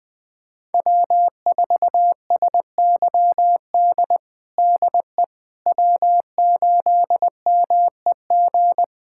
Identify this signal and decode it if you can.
Morse code, heard as a keyed tone, about 20 wpm, 700 Hz. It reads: W4SYD DE W8MEG
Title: USA Callsign DE Callsign - 5 Characters - 20wpm